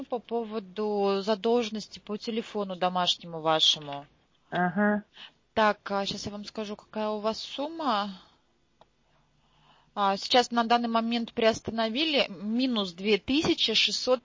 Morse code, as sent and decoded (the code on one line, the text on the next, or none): none